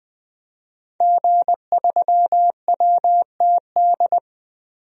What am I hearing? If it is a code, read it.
Morse code, heard as a keyed tone, about 20 wpm, 700 Hz. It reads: G3WTD
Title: DXCC-100 Call Signs - 5 Characters - 20wpm